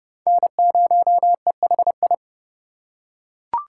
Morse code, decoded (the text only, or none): N0EHI